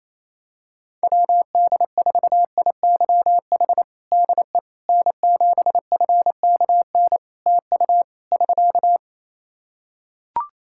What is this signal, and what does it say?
Morse code, heard as a keyed tone, about 28 wpm, 700 Hz. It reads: WD4IYH DE N7FKN TU <SK>
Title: Practice QSO - Exchange 4 - Ending - 28wpm